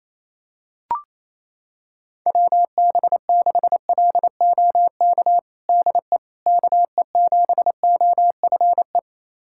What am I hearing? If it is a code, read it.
Morse code, heard as a keyed tone, about 27 wpm, 700 Hz. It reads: WB6LOK DE KE7OFE
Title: USA Callsign DE Callsign - 6 Characters - 28wpm